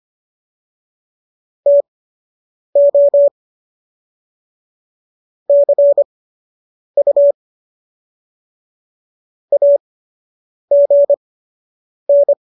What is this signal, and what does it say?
Morse code, heard as a keyed tone, about 25 wpm, spaced as for 8 wpm, 570 Hz. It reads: TO CU AGN